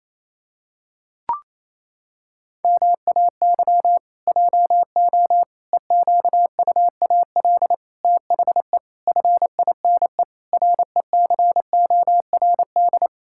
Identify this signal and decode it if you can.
Morse code, heard as a keyed tone, about 28 wpm, 700 Hz. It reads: MAY JO EQUAL THE FINE RECORD
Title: Pangram Challenge v2 - 28wpm